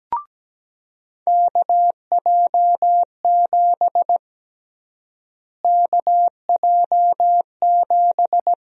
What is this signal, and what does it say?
Morse code, heard as a keyed tone, about 17 wpm, 700 Hz. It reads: KJ7 KJ7
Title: USA Call Sign Prefixes - Repeated 2x - 17wpm